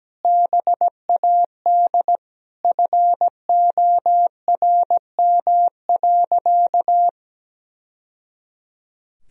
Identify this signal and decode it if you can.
Morse code, heard as a keyed tone, about 17 wpm, 700 Hz. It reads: BAD FORM.
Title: Sentences from Top 1000 Words - Review Part 1 - 17wpm